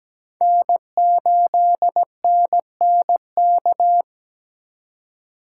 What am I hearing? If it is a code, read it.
Morse code, heard as a keyed tone, about 17 wpm, 700 Hz. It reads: N8NNK
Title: USA Call Signs - 5 Characters - 3x Character Spacing - 17wpm